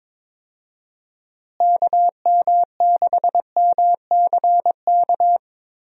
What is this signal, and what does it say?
Morse code, heard as a keyed tone, about 22 wpm, 700 Hz. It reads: KM6MCK
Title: USA Call Signs - 6 Characters - Repeated 2x - 22wpm